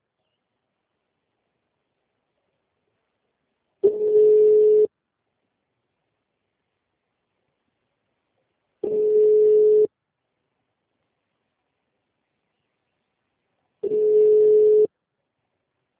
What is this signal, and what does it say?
Ringback tone (Europe, 425 Hz).
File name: out- -2018- - - .62769